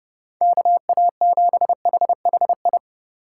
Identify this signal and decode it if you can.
Morse code, heard as a keyed tone, about 29 wpm, 700 Hz. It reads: KA7HHI